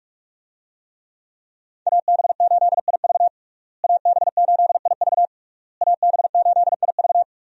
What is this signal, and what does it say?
Morse code, heard as a keyed tone, about 42 wpm, 700 Hz. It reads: AB8IV AB8IV AB8IV
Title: USA Call Signs - 5 Characters - Repeated 3x - 45wpm